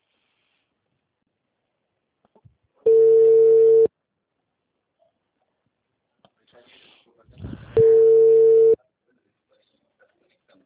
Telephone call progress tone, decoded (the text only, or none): ringback tone